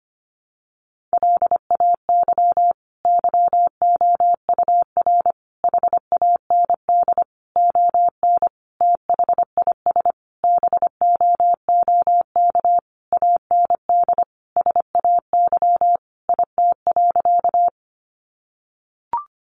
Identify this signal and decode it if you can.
Morse code, heard as a keyed tone, about 25 wpm, 700 Hz. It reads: LAY YOUR HAND ON THIS BOOK AND SAY IT.
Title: Sentences from Top 400 Words 25wpm